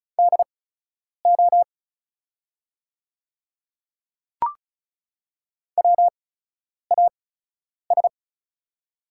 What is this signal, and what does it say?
Morse code, heard as a keyed tone, about 36 wpm, spaced as for 10 wpm, 700 Hz. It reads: DO WAS
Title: 8x Character Spacing - Top 500 Words - 35wpm